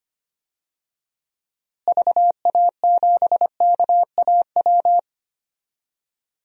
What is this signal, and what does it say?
Morse code, heard as a keyed tone, about 25 wpm, 700 Hz. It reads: VA7KAW
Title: DXCC-100 Call Signs - 6 Characters - Repeated 2x - 25wpm